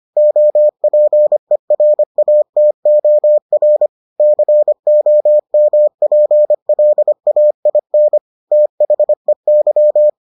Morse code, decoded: OPERATOR COMPLAIN THEY

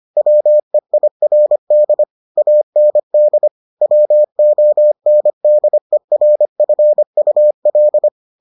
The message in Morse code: WEIRD AND WONDERFUL

